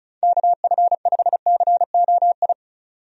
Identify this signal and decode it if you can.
Morse code, heard as a keyed tone, about 36 wpm, 700 Hz. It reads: KF5COI
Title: USA Call Signs - 6 Characters - 35wpm